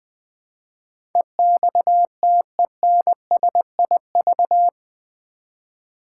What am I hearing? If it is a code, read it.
Morse code, heard as a keyed tone, about 20 wpm, 700 Hz. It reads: EXTENSIV
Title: Build A Word - 4 Letters At A Time - 11-Letter Words - 20wpm